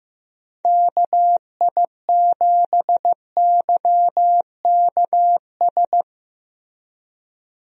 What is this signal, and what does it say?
Morse code, heard as a keyed tone, about 15 wpm, 700 Hz. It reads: KI7YKS
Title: USA Call Signs - 6 Characters - Repeated 3x - 15wpm